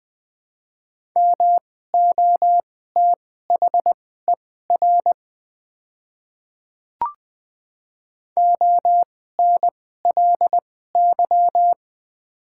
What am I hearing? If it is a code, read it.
Morse code, heard as a keyed tone, about 21 wpm, spaced as for 15 wpm, 700 Hz. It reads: MOTHER ONLY
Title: 2x Character Spacing - Top 200 Words - 20wpm